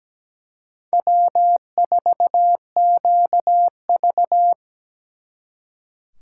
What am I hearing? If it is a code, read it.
Morse code, heard as a keyed tone, about 20 wpm, 700 Hz. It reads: W4QV
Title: Speed Racing - USA 1x2 Call Signs 17wpm